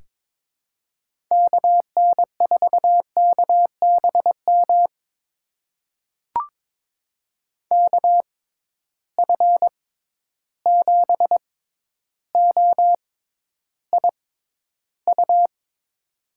Morse code, heard as a keyed tone, about 22 wpm, spaced as for 11 wpm, 700 Hz. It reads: KN4KBM KF7OIU
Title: USA Call Signs - 6 Characters - 6x Character Spacing - 22wpm